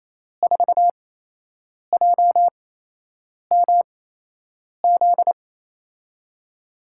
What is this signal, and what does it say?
Morse code, heard as a keyed tone, about 28 wpm, spaced as for 8 wpm, 700 Hz. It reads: 4JMZ
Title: USA Call Signs - 6 Characters - 8x Character Spacing - 28wpm